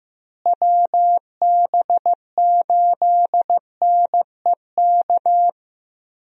Morse code, heard as a keyed tone, about 16 wpm, 700 Hz. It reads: WB8NEK
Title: USA Call Signs - 6 Characters - Repeated 2x - 15wpm